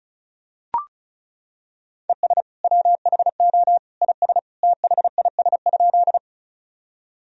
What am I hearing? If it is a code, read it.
Morse code, heard as a keyed tone, about 34 wpm, 700 Hz. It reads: ES WHO IS THIS?
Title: Sentences from Top 100 Words and Abbreviations 35wpm